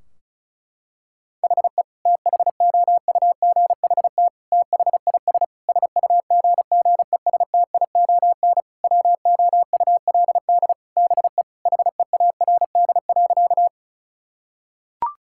Morse code, heard as a keyed tone, about 36 wpm, 700 Hz. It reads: HE THOUGHT THIS SUGGESTION WOULD BE HEARD.
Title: Sentences from Top 1600 Words 35wpm